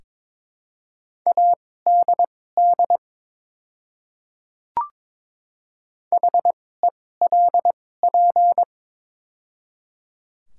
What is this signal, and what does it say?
Morse code, heard as a keyed tone, about 22 wpm, spaced as for 16 wpm, 700 Hz. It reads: ADD HELP
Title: 2x Character Spacing - Top 200 Words - 22wpm